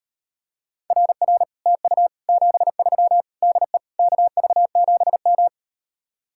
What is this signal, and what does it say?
Morse code, heard as a keyed tone, about 38 wpm, 700 Hz. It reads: RR TU 73 DE KV7M